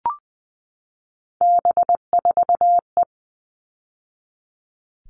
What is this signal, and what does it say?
Morse code, heard as a keyed tone, about 20 wpm, 700 Hz. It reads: B4E
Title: All Cut Numbers - 3-Digit 20wpm